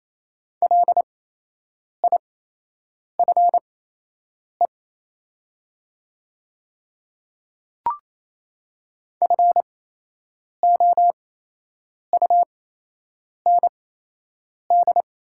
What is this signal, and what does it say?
Morse code, heard as a keyed tone, about 28 wpm, spaced as for 8 wpm, 700 Hz. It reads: LIFE FOUND
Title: 8x Character Spacing - Top 300 Words - 28wpm